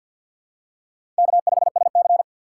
Morse code, heard as a keyed tone, about 47 wpm, 700 Hz. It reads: K5SC
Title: USA Call Signs - 4 Characters - 2x Character Spacing - 50wpm